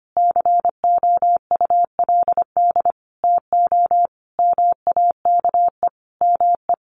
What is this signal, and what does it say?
Morse code, heard as a keyed tone, about 25 wpm, 700 Hz. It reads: COULD TO MAKE ME
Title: Sentences from Top 1500 Words 25wpm